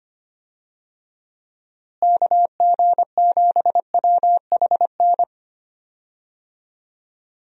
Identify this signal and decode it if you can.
Morse code, heard as a keyed tone, about 25 wpm, 700 Hz. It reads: KG7WHN
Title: USA Call Signs - 6 Characters - Repeated 2x - 25wpm